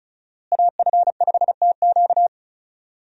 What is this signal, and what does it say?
Morse code, heard as a keyed tone, about 36 wpm, 700 Hz. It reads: AF5TQ
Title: USA Call Signs - 5 Characters - Repeated 2x - 35wpm